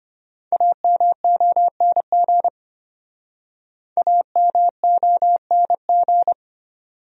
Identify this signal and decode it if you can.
Morse code, heard as a keyed tone, about 31 wpm, 700 Hz. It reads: AMONG AMONG
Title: Speed-Racing Top 500 Words 25wpm